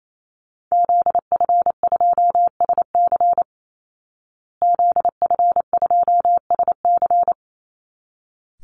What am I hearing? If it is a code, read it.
Morse code, heard as a keyed tone, about 28 wpm, 700 Hz. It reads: ZF2SC ZF2SC
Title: Call Signs Repeated 3x - 28wpm